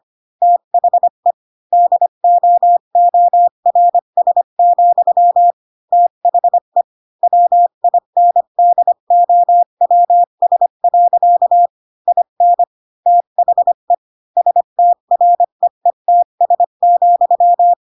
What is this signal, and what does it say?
Morse code, heard as a keyed tone, about 25 wpm, 700 Hz. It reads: THE DOORS, THE WINDOWS. IN THE STREETS,